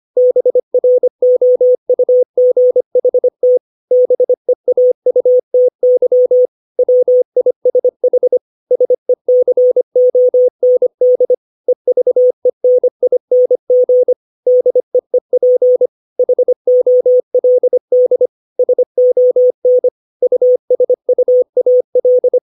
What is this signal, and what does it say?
Morse code, heard as a keyed tone, about 25 wpm, 500 Hz. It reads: BROUGHT BEAUTY WISH SECOND EVENING DEEP HOLD SON USUAL
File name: 632 words / 632 Words 25WPM